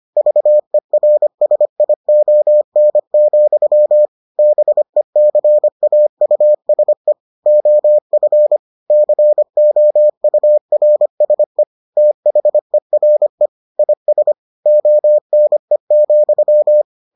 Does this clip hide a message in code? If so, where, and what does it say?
Yes: Morse code from 0.2 to 16.8 s, heard as a keyed tone, about 25 wpm, 600 Hz. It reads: VERSION, BECAUSE OF COURSE THERE IS ONE,